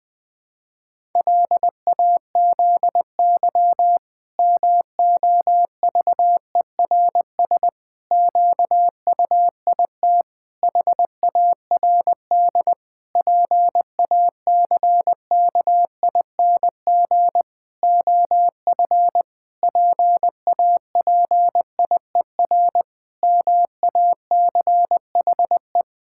Morse code, heard as a keyed tone, about 20 wpm, 700 Hz. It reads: LAZY MOVERS QUIT HARD PACKING OF PAPIER MACHE
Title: Pangram Challenge v2 - 20wpm